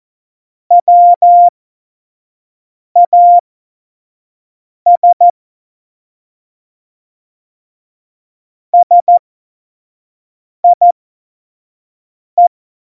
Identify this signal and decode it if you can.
Morse code, heard as a keyed tone, about 14 wpm, spaced as for 5 wpm, 700 Hz. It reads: WAS SIE
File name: Morsekurs / Witze / Witz 14. Januar 2020, 25 F70